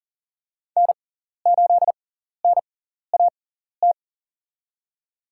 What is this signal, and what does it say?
Morse code, heard as a keyed tone, about 40 wpm, spaced as for 14 wpm, 700 Hz. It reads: N8NAT
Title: USA Call Signs - 5 Characters - 6x Character Spacing - 40wpm